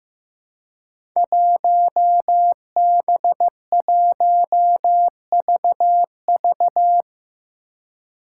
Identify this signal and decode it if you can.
Morse code, heard as a keyed tone, about 15 wpm, 700 Hz. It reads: 1B1VV
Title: Five Challenging Letter-Number Combinations 15wpm